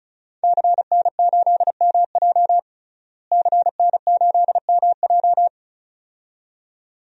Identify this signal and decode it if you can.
Morse code, heard as a keyed tone, about 36 wpm, 700 Hz. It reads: CN8MJ CN8MJ
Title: DXCC-100 Call Signs - 5 Characters - Repeated 2x - 35wpm